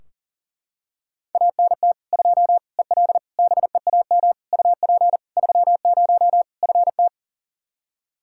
Morse code, heard as a keyed tone, about 40 wpm, 700 Hz. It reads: ANT 2 EL BEAM UP 30 FT
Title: Practice QSO - Exchange 2 - Info - 40wpm